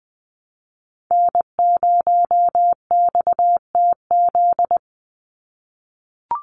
Morse code, heard as a keyed tone, about 20 wpm, 700 Hz. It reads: N0XTZ
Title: USA Call Signs 20wpm